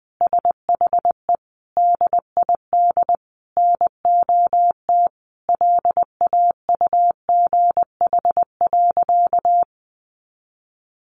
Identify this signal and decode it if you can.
Morse code, heard as a keyed tone, about 20 wpm, 700 Hz. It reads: SHE DID NOT LAUGH.
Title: Sentences from Top 1100 Words 20wpm